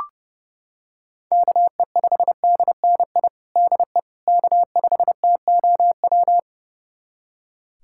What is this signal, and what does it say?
Morse code, heard as a keyed tone, about 30 wpm, 700 Hz. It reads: KE5DNI DE K5TOW
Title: USA Callsign DE Callsign 30wpm